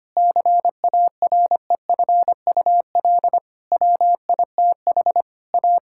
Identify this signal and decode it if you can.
Morse code, heard as a keyed tone, about 25 wpm, 700 Hz. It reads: CAREFUL WITH A
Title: Sentences from Top 1600 Words 25wpm